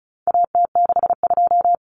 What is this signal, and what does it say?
Morse code, heard as a keyed tone, about 34 wpm, 700 Hz. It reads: AT62
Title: Common Cut Numbers - 4-Digit 35wpm